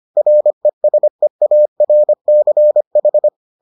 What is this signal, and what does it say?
Morse code, heard as a keyed tone, about 25 wpm, 600 Hz. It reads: RESEARCH